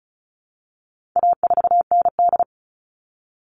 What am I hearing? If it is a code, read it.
Morse code, heard as a keyed tone, about 34 wpm, 700 Hz. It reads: A4ND
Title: All Cut Numbers - 4-Digit 35wpm